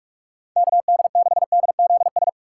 Morse code, heard as a keyed tone, about 46 wpm, 700 Hz. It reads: KD6DZS
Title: USA Call Signs - 6 Characters - 2x Character Spacing - 45wpm